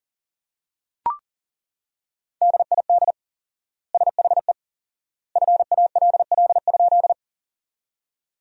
Morse code, heard as a keyed tone, about 37 wpm, 700 Hz. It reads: DID SHE FALL?
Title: 4x Word-Spacing - Sentences from Top 500 Words - 40wpm